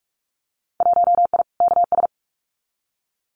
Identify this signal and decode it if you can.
Morse code, heard as a keyed tone, about 46 wpm, 700 Hz. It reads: 1I KS
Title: Winter Field Day Exchanges Encore 45wpm